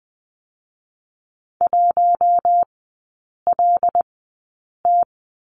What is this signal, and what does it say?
Morse code, heard as a keyed tone, about 20 wpm, spaced as for 8 wpm, 700 Hz. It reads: 1LT
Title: 081 Lesson 14e - Number 2 - Three Characters 20wpm